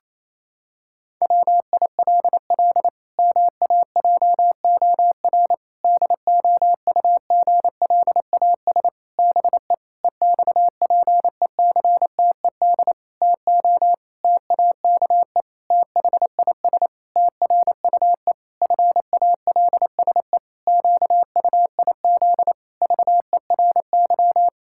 Morse code, heard as a keyed tone, about 28 wpm, 700 Hz. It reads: WILL MAJOR DOUGLAS BE EXPECTED TO TAKE THIS TRUE FALSE QUIZ VERY
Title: Pangram Challenge v2 - 28wpm